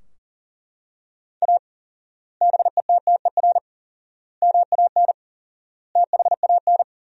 Morse code, heard as a keyed tone, about 40 wpm, 700 Hz. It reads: A BETTER MAN THAN
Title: 4x Word-Spacing - Sentences from Top 400 Words - 40wpm